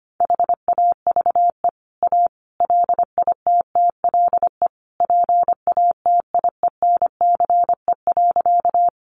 Morse code, heard as a keyed tone, about 25 wpm, 700 Hz. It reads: HAVE A LITTLE PATIENCE.
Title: Sentences from Top 1300 Words 25wpm